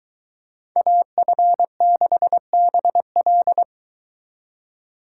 Morse code, heard as a keyed tone, about 22 wpm, 700 Hz. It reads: AF6BL